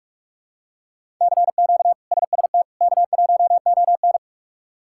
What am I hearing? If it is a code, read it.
Morse code, heard as a keyed tone, about 46 wpm, 700 Hz. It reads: CQ SST K1ON
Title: K1USN SST QSO Practice - 45wpm